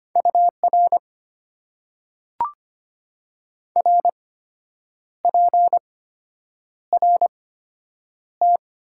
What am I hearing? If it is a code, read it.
Morse code, heard as a keyed tone, about 25 wpm, spaced as for 8 wpm, 690 Hz. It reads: UR RPRT
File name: Top 90 QSO Elements - 8x Character Spacing - 25wpm